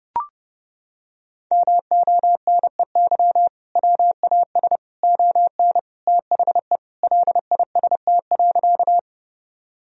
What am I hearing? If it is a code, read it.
Morse code, heard as a keyed tone, about 30 wpm, 700 Hz. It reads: MONEY WAS ON THE LIST.
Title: Sentences from Top 1000 Words - Review Part 2 - 30wpm